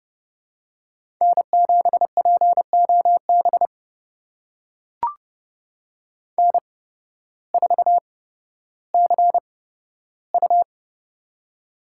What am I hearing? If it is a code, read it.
Morse code, heard as a keyed tone, about 30 wpm, spaced as for 14 wpm, 700 Hz. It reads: N7POB N4CU
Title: USA Call Signs - 8x Character Spacing - 30wpm